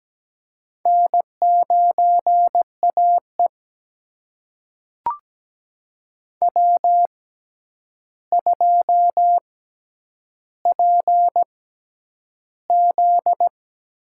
Morse code, heard as a keyed tone, about 17 wpm, spaced as for 9 wpm, 700 Hz. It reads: N9AE W2PZ
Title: USA Call Signs - 6x Character Spacing - 17wpm